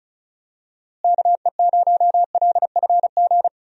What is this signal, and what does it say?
Morse code, heard as a keyed tone, about 36 wpm, 700 Hz. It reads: KE0LFG
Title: USA Call Signs - 6 Characters - Repeated 2x - 35wpm